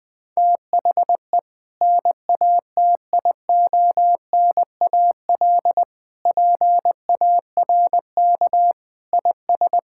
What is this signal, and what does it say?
Morse code, heard as a keyed tone, about 21 wpm, 700 Hz. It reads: THE NATIONAL PARK IS
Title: Sentences with 2-Word Phrases - 20wpm